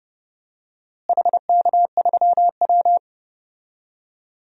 Morse code, heard as a keyed tone, about 31 wpm, 700 Hz. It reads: HK3W
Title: DXCC-100 Call Signs - 4 Characters - Repeated 3x - 30wpm